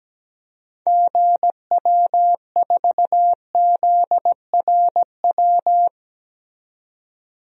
Morse code, heard as a keyed tone, about 17 wpm, 700 Hz. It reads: GW4ZRW
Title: DXCC-100 Call Signs - 6 Characters - 17wpm